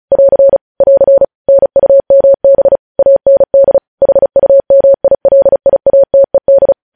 Morse code, heard as a keyed tone, about 36 wpm, 550 Hz. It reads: <AR> <AR> NUMB AND HUMILIATED
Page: News headlines in Morse code at 35 wpm